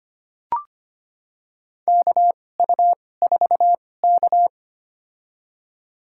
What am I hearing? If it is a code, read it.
Morse code, heard as a keyed tone, about 25 wpm, spaced as for 18 wpm, 700 Hz. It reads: KU4K
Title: USA Call Signs - 2x Character Spacing - 25wpm